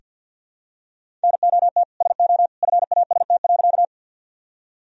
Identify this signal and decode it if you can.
Morse code, heard as a keyed tone, about 47 wpm, 700 Hz. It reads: NOT SO FAST.
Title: Sentences from Top 1000 Words - Review Part 2 - 50wpm